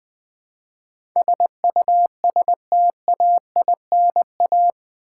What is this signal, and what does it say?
Morse code, heard as a keyed tone, about 20 wpm, 700 Hz. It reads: SUSTAINA
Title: Build A Word - 4 Letters At A Time - 11-Letter Words - 20wpm